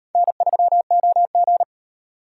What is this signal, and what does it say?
Morse code, heard as a keyed tone, about 34 wpm, 700 Hz. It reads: N3OG